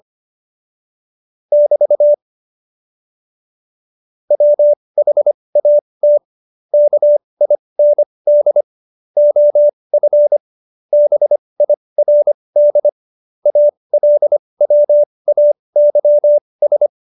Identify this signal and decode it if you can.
Morse code, heard as a keyed tone, about 25 wpm, spaced as for 20 wpm, 590 Hz. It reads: <BT> WHAT KIND OF BIRD ALWAYS
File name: Jokes-25-char-20-eff